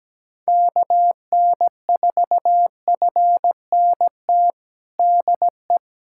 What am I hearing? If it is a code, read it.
Morse code, heard as a keyed tone, about 17 wpm, 700 Hz. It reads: KN4FNT DE
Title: USA Callsign DE Callsign - 6 Characters - 17wpm